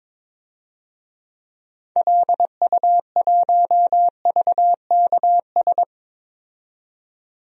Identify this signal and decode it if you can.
Morse code, heard as a keyed tone, about 22 wpm, 700 Hz. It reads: LU1VKS
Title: DXCC-100 Call Signs - 6 Characters - 22wpm